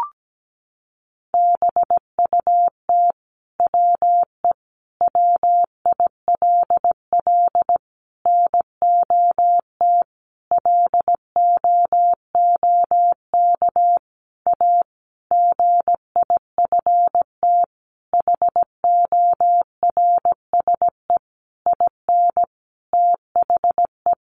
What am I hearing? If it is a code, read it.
Morse code, heard as a keyed tone, about 17 wpm, 700 Hz. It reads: BUT WE WILL NOT LOOK A GIFT HORSE IN THE
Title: Sentences from Top 1500 Words 17wpm